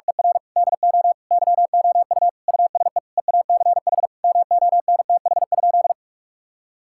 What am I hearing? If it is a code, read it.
Morse code, heard as a keyed tone, about 46 wpm, 700 Hz. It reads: ER DO YOU USE EACH MONTH?
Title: Sentences from Top 1000 Words - Review Part 6 - 45wpm